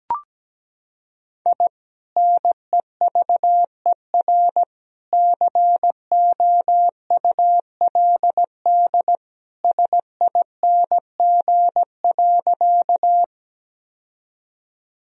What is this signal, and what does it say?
Morse code, heard as a keyed tone, about 17 wpm, 700 Hz. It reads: I NEVER COULD SING.
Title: Sentences from Top 1000 Words - Review Part 2 - 17wpm